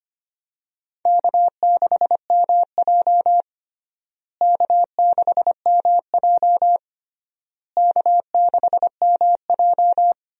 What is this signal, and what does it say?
Morse code, heard as a keyed tone, about 25 wpm, 700 Hz. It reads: K6MJ K6MJ K6MJ
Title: USA Call Signs - 4 Characters - Repeated 3x - 25wpm